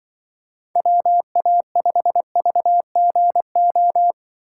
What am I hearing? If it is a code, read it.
Morse code, heard as a keyed tone, about 25 wpm, 700 Hz. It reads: WA5VGO
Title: Speed Racing - USA 2x3 Call Signs 20wpm